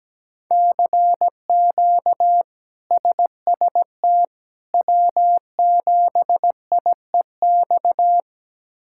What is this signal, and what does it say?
Morse code, heard as a keyed tone, about 17 wpm, 700 Hz. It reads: CQ SST W7IEX
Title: K1USN SST QSO Practice - 17wpm